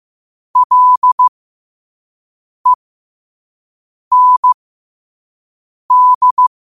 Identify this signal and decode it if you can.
Morse code, heard as a keyed tone, about 15 wpm, spaced as for 5 wpm, 1000 Hz. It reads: LEND
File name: qotdfiles / 5wpm